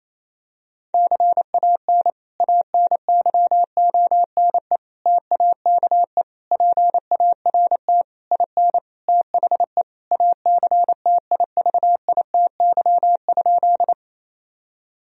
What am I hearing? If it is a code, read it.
Morse code, heard as a keyed tone, about 28 wpm, 700 Hz. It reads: CAN ANYONE TAKE PART IN THE ACTIVITY?
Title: Sentences from Top 1000 Words - Review Part 4 - 28wpm